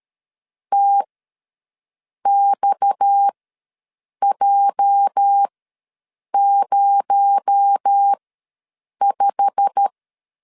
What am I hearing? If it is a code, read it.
Morse code, heard as a keyed tone, about 13 wpm, spaced as for 7 wpm, 800 Hz. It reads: TXJ05
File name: LEZIONI / 10 SIGNS / 10C DOT ws 12wpm 3548